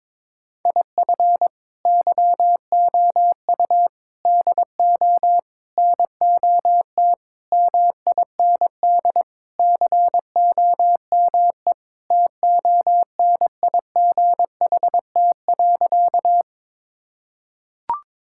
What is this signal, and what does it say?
Morse code, heard as a keyed tone, about 22 wpm, 700 Hz. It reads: IF YOU DO NOT MIND COME TONIGHT.
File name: practice / 22wpm / Sentences with 4-Word Phrases - 22wpm